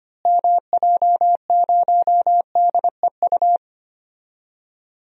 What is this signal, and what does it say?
Morse code, heard as a keyed tone, about 25 wpm, 700 Hz. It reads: MJ0DEU
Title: DXCC-100 Call Signs - 6 Characters - Repeated 2x - 25wpm